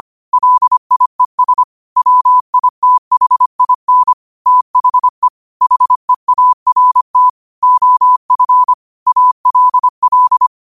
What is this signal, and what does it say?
Morse code, heard as a keyed tone, about 25 wpm, 1000 Hz. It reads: LIES WITHIN THE HEART OF ALL
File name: qotdfiles / 25wpm